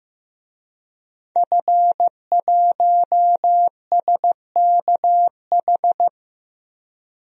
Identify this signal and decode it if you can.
Morse code, heard as a keyed tone, about 15 wpm, 700 Hz. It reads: F1SKH
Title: DXCC-100 Call Signs - 5 Characters - Repeated 3x - 15wpm